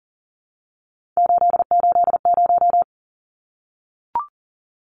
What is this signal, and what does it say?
Morse code, heard as a keyed tone, about 40 wpm, 700 Hz. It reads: 880
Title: Numbers - 3-Digit 40wpm